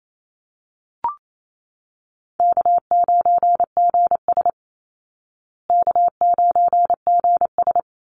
Morse code, heard as a keyed tone, about 28 wpm, 700 Hz. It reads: K9GS K9GS
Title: Easy Call Signs Repeated 3x - 28wpm